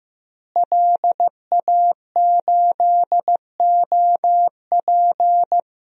Morse code, heard as a keyed tone, about 15 wpm, 700 Hz. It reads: LA8OP
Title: DXCC-100 Call Signs - 5 Characters - Repeated 3x - 15wpm